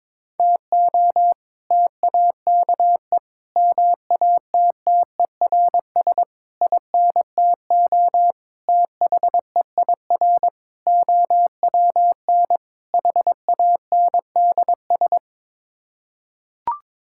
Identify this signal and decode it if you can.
Morse code, heard as a keyed tone, about 22 wpm, 700 Hz. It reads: TO TAKE MATTERS INTO THEIR OWN HANDS
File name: practice / 22wpm / Sets of 7 Words - 22wpm